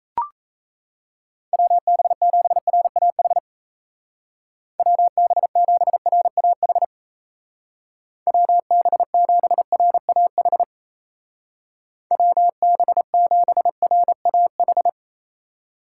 Morse code, heard as a keyed tone, about 34 wpm, 700 Hz. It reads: WB7RAH WB7RAH WB7RAH WB7RAH
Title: Speed Racing - USA 2x3 Call Signs 28wpm